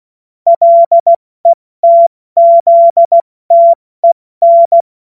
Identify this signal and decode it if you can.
Morse code, heard as a keyed tone, about 16 wpm, 680 Hz. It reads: LETZTEN
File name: Morsekurs / Witze / Witz 24. Januar 2023 70 F80